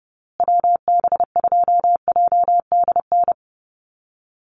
Morse code, heard as a keyed tone, about 30 wpm, 700 Hz. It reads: WB2JDN
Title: USA Call Signs Repeated 2x 30wpm